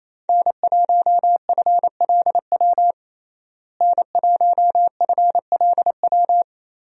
Morse code, heard as a keyed tone, about 27 wpm, 700 Hz. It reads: N1FLW N1FLW
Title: USA Call Signs - 5 Characters - Repeated 3x - 28wpm